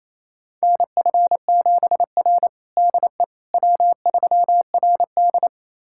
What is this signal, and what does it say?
Morse code, heard as a keyed tone, about 28 wpm, 700 Hz. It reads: NF7R DE W3RD
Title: USA Callsign DE Callsign - 4 Characters - 28wpm